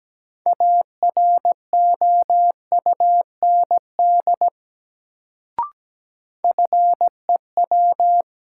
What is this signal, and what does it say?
Morse code, heard as a keyed tone, about 17 wpm, 700 Hz. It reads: AROUND FEW
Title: Sets of 2 Words - New Word Drill - 17wpm